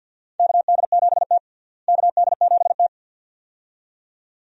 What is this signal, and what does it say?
Morse code, heard as a keyed tone, about 50 wpm, 700 Hz. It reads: KD7T KD7T
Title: USA Call Signs - 4 Characters - Repeated 2x - 50wpm